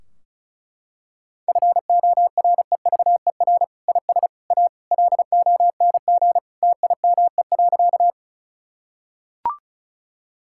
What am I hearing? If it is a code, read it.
Morse code, heard as a keyed tone, about 36 wpm, 700 Hz. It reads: FOREVER IS A LONG TIME.
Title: Sentences from Top 1600 Words 35wpm